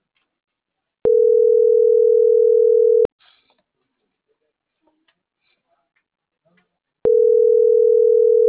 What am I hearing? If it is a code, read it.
Ringback tone (North America), 440 + 480 Hz.